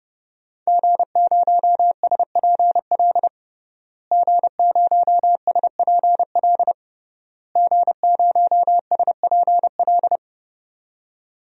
Morse code, heard as a keyed tone, about 32 wpm, 700 Hz. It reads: G0SPL G0SPL G0SPL
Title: DXCC-100 Call Signs - 5 Characters - Repeated 3x - 30wpm